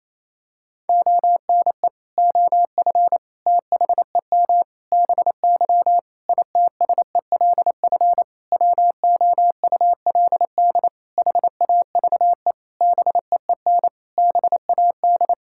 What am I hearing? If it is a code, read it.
Morse code, heard as a keyed tone, about 27 wpm, 700 Hz. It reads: ONE OF THEM BY ITSELF WOULD HAVE BEEN BAD
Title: Sentences from Top 1000 Words - Review Part 5 - 28wpm